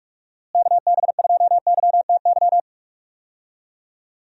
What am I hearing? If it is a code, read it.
Morse code, heard as a keyed tone, about 47 wpm, 700 Hz. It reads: KB2YTY
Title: USA Call Signs - 6 Characters - Repeated 3x - 45wpm